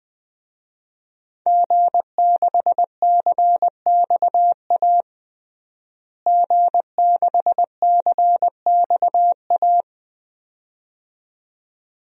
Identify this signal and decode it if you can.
Morse code, heard as a keyed tone, about 20 wpm, 700 Hz. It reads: G6CXA G6CXA
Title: DXCC-100 Call Signs - 5 Characters - Repeated 2x - 20wpm